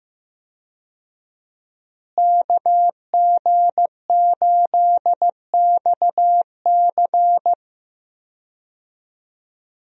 Morse code, heard as a keyed tone, about 15 wpm, 700 Hz. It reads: KG8XC
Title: USA Call Signs - 5 Characters - Repeated 3x - 15wpm